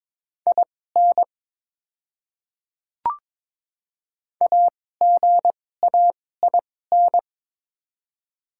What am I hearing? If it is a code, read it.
Morse code, heard as a keyed tone, about 22 wpm, spaced as for 16 wpm, 700 Hz. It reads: IN AGAIN
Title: 2x Character Spacing - Top 200 Words - 22wpm